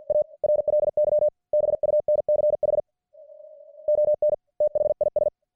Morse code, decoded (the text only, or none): A RLY BUNCH ON THIS